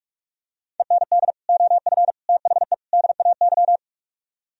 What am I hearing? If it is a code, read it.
Morse code, heard as a keyed tone, about 44 wpm, 700 Hz. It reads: END OF THE DAY